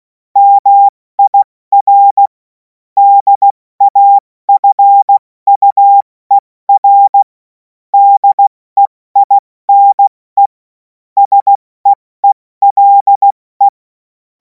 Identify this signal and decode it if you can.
Morse code, heard as a keyed tone, about 16 wpm, 800 Hz. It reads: MIR DAFUER DEINE SEELE
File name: Morsekurs / Witze / Witz 30. Juni 2026 70 F80